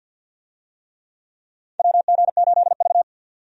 Morse code, heard as a keyed tone, about 53 wpm, 700 Hz. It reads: WG8V
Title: USA Call Signs - 4 Characters - 50wpm